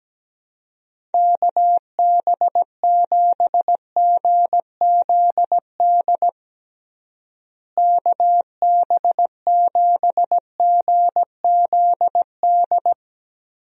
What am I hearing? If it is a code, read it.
Morse code, heard as a keyed tone, about 17 wpm, 700 Hz. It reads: KB7GZD KB7GZD
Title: USA Call Signs - 6 Characters - Repeated 2x - 17wpm